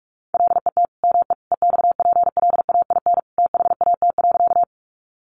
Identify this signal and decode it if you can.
Morse code, heard as a keyed tone, about 44 wpm, 700 Hz. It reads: LET ME EXPLAIN THAT.
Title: Sentences from Top 1100 Words 45wpm